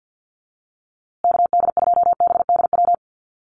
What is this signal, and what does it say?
Morse code, heard as a keyed tone, about 50 wpm, 700 Hz. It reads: KD2BDW
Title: USA Call Signs Repeated 3x 50wpm